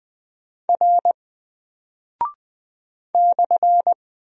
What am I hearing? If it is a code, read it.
Morse code, heard as a keyed tone, about 20 wpm, 700 Hz. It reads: R /
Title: Single Character (Letter:Number:Punctuation:Prosign) 20wpm